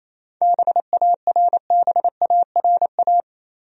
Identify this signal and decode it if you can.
Morse code, heard as a keyed tone, about 28 wpm, 700 Hz. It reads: BARBARA
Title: USA Common Names - 28wpm